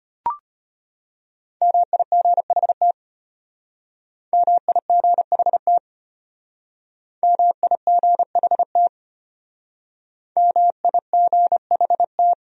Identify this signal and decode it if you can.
Morse code, heard as a keyed tone, about 31 wpm, 700 Hz. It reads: MIGHT MIGHT MIGHT MIGHT
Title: Speed-Racing Top 500 Words 25wpm